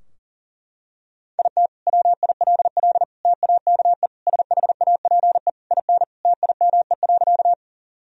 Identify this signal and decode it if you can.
Morse code, heard as a keyed tone, about 40 wpm, 700 Hz. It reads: IT WILL TAKE SHAPE IN TIME.
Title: Sentences from Top 1000 Words - Review Part 3 - 40wpm